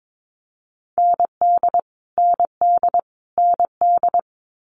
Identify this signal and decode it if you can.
Morse code, heard as a keyed tone, about 22 wpm, 700 Hz. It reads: ND ND ND
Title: Multi-letter Phonemes Repeated 3x 22wpm